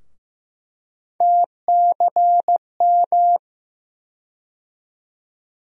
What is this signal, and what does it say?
Morse code, heard as a keyed tone, about 15 wpm, 700 Hz. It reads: TCM
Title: USA Call Sign Postfixes - 3 Letters 15wpm